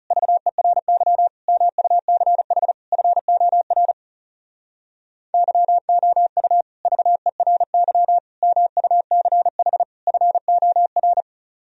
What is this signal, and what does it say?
Morse code, heard as a keyed tone, about 37 wpm, 700 Hz. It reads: VERY MUCH FOR YOU VERY MUCH FOR
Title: Speed-Racing - Sets of 4 Words 30wpm